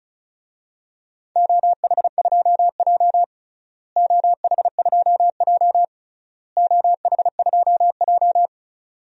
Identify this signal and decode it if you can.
Morse code, heard as a keyed tone, about 34 wpm, 700 Hz. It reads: OH2J OH2J OH2J
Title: DXCC-100 Call Signs - 4 Characters - Repeated 3x - 35wpm